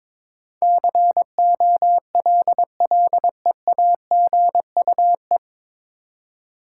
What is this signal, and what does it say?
Morse code, heard as a keyed tone, about 22 wpm, 690 Hz. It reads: COLLEAGUE
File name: 9 Letter Words - First 3 Letters Given - 22wpm